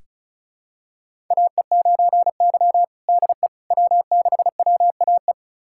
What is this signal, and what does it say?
Morse code, heard as a keyed tone, about 34 wpm, 700 Hz. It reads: AE9Y DE W6WAE
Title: USA Callsign DE Callsign 35wpm